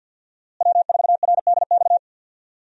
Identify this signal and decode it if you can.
Morse code, heard as a keyed tone, about 49 wpm, 700 Hz. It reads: W4RDX
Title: USA Call Signs - 5 Characters - Repeated 2x - 50wpm